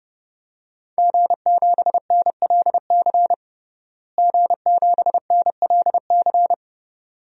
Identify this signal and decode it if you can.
Morse code, heard as a keyed tone, about 32 wpm, 700 Hz. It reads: G7NLC G7NLC
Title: DXCC-100 Call Signs - 5 Characters - Repeated 2x - 30wpm